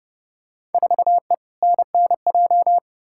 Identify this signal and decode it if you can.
Morse code, heard as a keyed tone, about 30 wpm, 700 Hz. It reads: 4E NNJ
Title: Field Day Exchanges - 30wpm